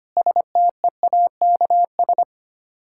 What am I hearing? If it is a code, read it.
Morse code, heard as a keyed tone, about 25 wpm, 690 Hz. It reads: STEAKS
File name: practice / 25wpm / Difficult Words with Hints - 25wpm